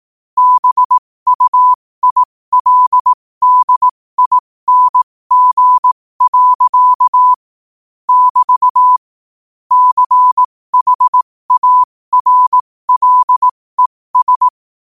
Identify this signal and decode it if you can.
Morse code, heard as a keyed tone, about 18 wpm, spaced as for 15 wpm, 1000 Hz. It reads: BUILDING. <BT> CHARLES
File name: qotdfiles / 15wpm